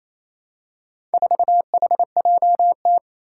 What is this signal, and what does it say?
Morse code, heard as a keyed tone, about 28 wpm, 700 Hz. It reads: 4HJT
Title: Four Challenging Letter-Number Combinations 28wpm